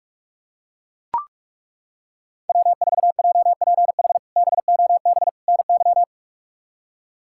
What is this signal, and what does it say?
Morse code, heard as a keyed tone, about 41 wpm, 700 Hz. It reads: W4JPH BOB NY
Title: K1USN SST QSO Practice - 45wpm